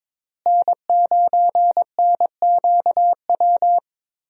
Morse code, heard as a keyed tone, about 22 wpm, 700 Hz. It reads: N9NQW
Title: USA Call Signs - 5 Characters - Repeated 3x - 22wpm